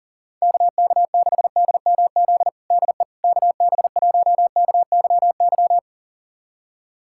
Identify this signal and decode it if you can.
Morse code, heard as a keyed tone, about 39 wpm, 700 Hz. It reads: KK6DMZ DE KB1KYY
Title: USA Callsign DE Callsign - 6 Characters - 40wpm